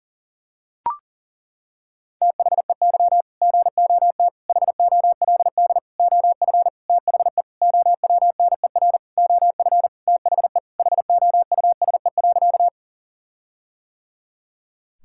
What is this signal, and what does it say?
Morse code, heard as a keyed tone, about 37 wpm, 700 Hz. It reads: THEY GOT HOLD OF THE OWNER OF THE HOUSE.
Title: Sentences from Top 1600 Words 40wpm